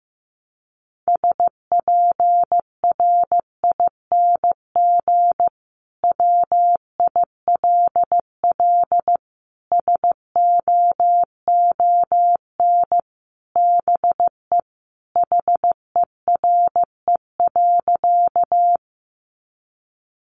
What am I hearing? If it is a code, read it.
Morse code, heard as a keyed tone, about 15 wpm, 700 Hz. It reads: SPRING WILL SOON BE HERE.
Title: Sentences from Top 1200 Words 15wpm